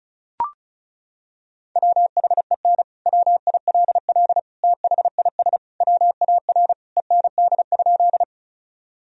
Morse code, heard as a keyed tone, about 36 wpm, 700 Hz. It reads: WHEN WILL THIS WAR END?
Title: Sentences from Top 1000 Words - Review Part 2 - 35wpm